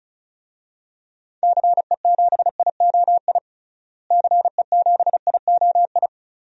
Morse code, heard as a keyed tone, about 34 wpm, 700 Hz. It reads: CE7IOI CE7IOI
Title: DXCC-100 Call Signs - 6 Characters - Repeated 3x - 35wpm